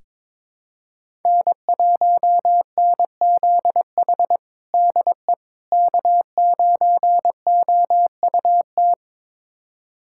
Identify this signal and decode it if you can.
Morse code, heard as a keyed tone, about 22 wpm, 700 Hz. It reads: N1NZH DE K9OUT
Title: USA Callsign DE Callsign - 5 Characters - 22wpm